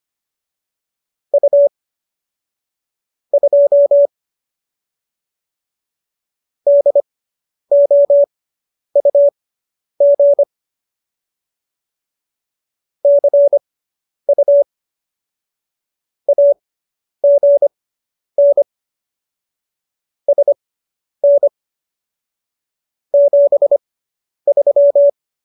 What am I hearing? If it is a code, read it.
Morse code, heard as a keyed tone, about 25 wpm, spaced as for 10 wpm, 570 Hz. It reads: U 2 DOUG CU AGN SN 73